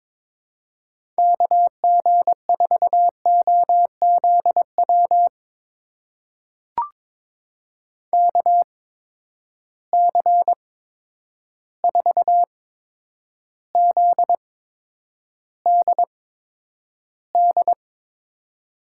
Morse code, heard as a keyed tone, about 22 wpm, spaced as for 9 wpm, 700 Hz. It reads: KG4OZW KC4ZDD
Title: USA Call Signs - 6 Characters - 8x Character Spacing - 22wpm